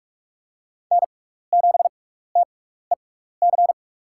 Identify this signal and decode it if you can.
Morse code, heard as a keyed tone, about 48 wpm, spaced as for 16 wpm, 700 Hz. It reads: N7TEC
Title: USA Call Signs - 5 Characters - 6x Character Spacing - 45wpm